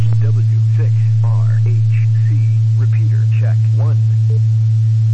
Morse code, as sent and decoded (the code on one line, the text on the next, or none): none